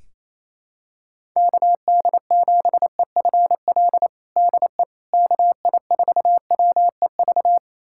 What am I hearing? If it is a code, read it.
Morse code, heard as a keyed tone, about 28 wpm, 700 Hz. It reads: KD7EFL DE KI4WEV
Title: USA Callsign DE Callsign - 6 Characters - 28wpm